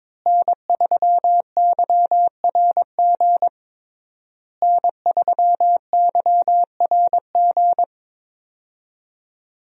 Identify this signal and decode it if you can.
Morse code, heard as a keyed tone, about 22 wpm, 700 Hz. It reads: N3YRG N3YRG
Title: USA Call Signs - 5 Characters - Repeated 3x - 22wpm